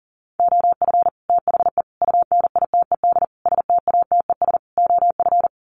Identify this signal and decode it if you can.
Morse code, heard as a keyed tone, about 40 wpm, 700 Hz. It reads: OF THE UNITED STATES OF
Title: Sets of 5 Words Encore 40wpm